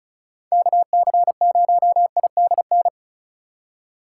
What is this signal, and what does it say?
Morse code, heard as a keyed tone, about 39 wpm, 700 Hz. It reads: KC0IDN
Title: Speed Racing - USA 2x3 Call Signs 35wpm